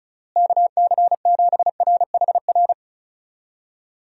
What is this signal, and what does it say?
Morse code, heard as a keyed tone, about 36 wpm, 700 Hz. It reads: KC7RHR
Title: USA Call Signs - 6 Characters - Repeated 3x - 35wpm